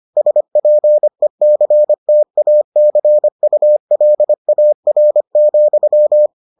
Morse code, heard as a keyed tone, about 25 wpm, 600 Hz. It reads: SPECTACULAR,